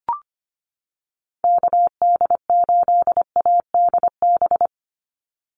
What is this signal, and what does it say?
Morse code, heard as a keyed tone, about 25 wpm, 700 Hz. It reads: KD8ADB
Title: USA Call Signs 25wpm